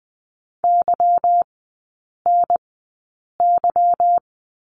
Morse code, heard as a keyed tone, about 20 wpm, spaced as for 8 wpm, 690 Hz. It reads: YNY
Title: 145 Lesson 24e - Letter y - Three Characters 20wpm